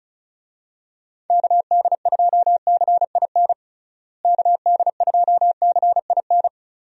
Morse code, heard as a keyed tone, about 34 wpm, 700 Hz. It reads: KD2CIN KD2CIN
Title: USA Call Signs - 6 Characters - Repeated 3x - 35wpm